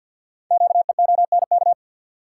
Morse code, heard as a keyed tone, about 51 wpm, 700 Hz. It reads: QEONK
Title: Code Groups - 5 Letters - 50wpm